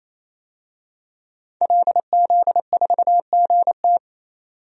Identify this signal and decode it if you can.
Morse code, heard as a keyed tone, about 28 wpm, 700 Hz. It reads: LZ4GT
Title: DXCC-100 Call Signs - 5 Characters - 28wpm